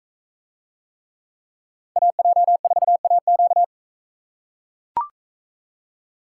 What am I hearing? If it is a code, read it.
Morse code, heard as a keyed tone, about 40 wpm, 700 Hz. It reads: AJ4AQ